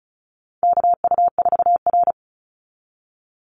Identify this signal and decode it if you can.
Morse code, heard as a keyed tone, about 34 wpm, 700 Hz. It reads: KU4R
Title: NA SOTA Chaser Callsigns 35wpm